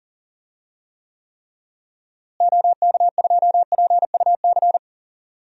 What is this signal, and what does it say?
Morse code, heard as a keyed tone, about 38 wpm, 700 Hz. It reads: OK2PUC
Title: DXCC-100 Call Signs - 6 Characters - Repeated 3x - 40wpm